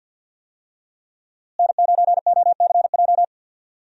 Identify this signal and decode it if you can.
Morse code, heard as a keyed tone, about 52 wpm, 700 Hz. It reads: N9OCJ
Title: USA Call Signs - 5 Characters - Repeated 3x - 50wpm